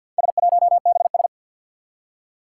Morse code, heard as a keyed tone, about 47 wpm, 700 Hz. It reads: S1BS
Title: Four Challenging Letter-Number Combinations 50wpm